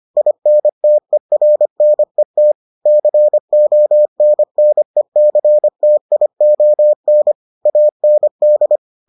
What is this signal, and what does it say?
Morse code, heard as a keyed tone, about 25 wpm, 600 Hz. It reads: INTERNET CONNECTION AND